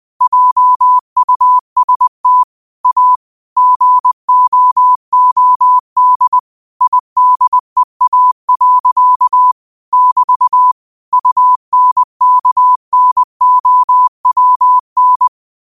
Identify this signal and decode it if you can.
Morse code, heard as a keyed tone, about 20 wpm, 1000 Hz. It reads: JUST A GOOD IDEA. <BT> UNKNOWN
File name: qotdfiles / 20wpm